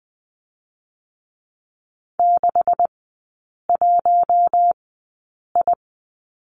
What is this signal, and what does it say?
Morse code, heard as a keyed tone, about 20 wpm, spaced as for 8 wpm, 700 Hz. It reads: 61I
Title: 127 Lesson 21e - Number 6 - Three Characters 20wpm